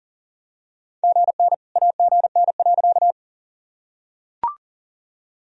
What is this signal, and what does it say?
Morse code, heard as a keyed tone, about 40 wpm, 700 Hz. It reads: GN AGN.
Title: Sentences from Top 300 Words and Abbreviations 40wpm